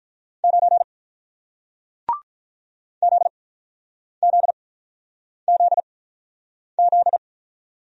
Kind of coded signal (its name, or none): Morse code